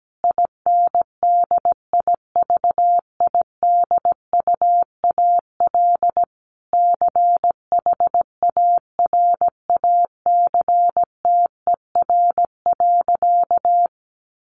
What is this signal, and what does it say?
Morse code, heard as a keyed tone, about 17 wpm, 700 Hz. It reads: INDIVIDUAL CHARACTER.
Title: Sentences from Top 1500 Words 17wpm